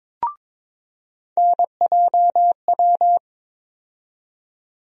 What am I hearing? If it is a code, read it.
Morse code, heard as a keyed tone, about 22 wpm, 700 Hz. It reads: NJW
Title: USA Call Sign Postfixes - 3 Letters 22wpm